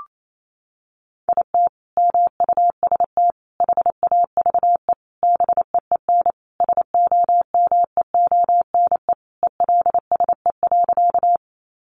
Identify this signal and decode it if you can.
Morse code, heard as a keyed tone, about 28 wpm, 700 Hz. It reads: IT MUST HAVE BEEN SOMEONE ELSE.
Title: Sentences from Top 800 Words 28wpm